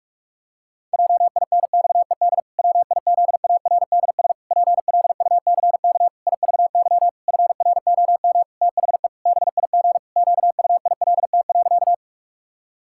Morse code, heard as a keyed tone, about 46 wpm, 700 Hz. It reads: JINXED WIZARDS PLUCK IVY FROM THE BIG QUILT.
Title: Pangram Challenge v2 - 45wpm